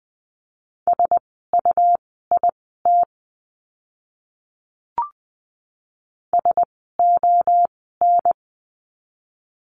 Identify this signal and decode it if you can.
Morse code, heard as a keyed tone, about 20 wpm, spaced as for 14 wpm, 700 Hz. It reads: SUIT SON
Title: 101 Lesson 17f - Letter u - Words 20wpm